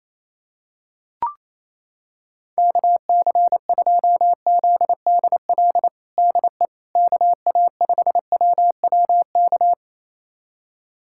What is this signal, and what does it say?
Morse code, heard as a keyed tone, about 28 wpm, 700 Hz. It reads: KC2ZDL DE KA5WWK
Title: USA Callsign DE Callsign 28wpm